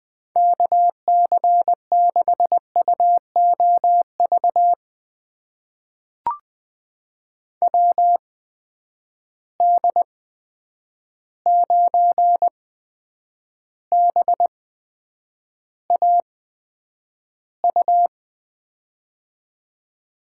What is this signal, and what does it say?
Morse code, heard as a keyed tone, about 20 wpm, spaced as for 9 wpm, 700 Hz. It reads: KC6UOV WD9BAU
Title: USA Call Signs - 6 Characters - 8x Character Spacing - 20wpm